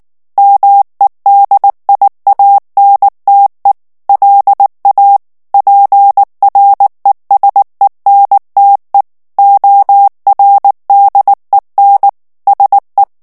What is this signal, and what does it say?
Morse code, heard as a keyed tone, about 19 wpm, 800 Hz. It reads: MEDIANTE LA PRESENTE ORDEN SE
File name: morse mp3 / ESPAGNOL TEXTE EN CLAIR 2 VITESSE 18 MOTS MINUTES